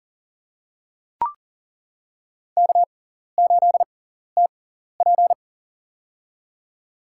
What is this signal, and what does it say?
Morse code, heard as a keyed tone, about 40 wpm, spaced as for 14 wpm, 700 Hz. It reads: K8TP
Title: USA Call Signs - 4 Characters - 6x Character Spacing - 40wpm